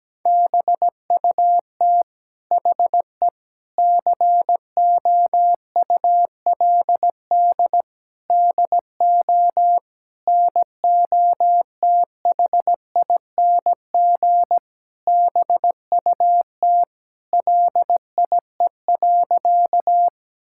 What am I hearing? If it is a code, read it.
Morse code, heard as a keyed tone, about 17 wpm, 700 Hz. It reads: BUT HE COULD DO NOTHING BUT LIE.
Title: Sentences from Top 1600 Words 17wpm